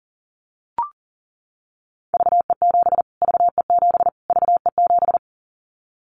Morse code, heard as a keyed tone, about 40 wpm, 700 Hz. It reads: VE7 VE7 VE7
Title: Callsign Prefixes for Top 25 DXCC - Repeated 3x - 40wpm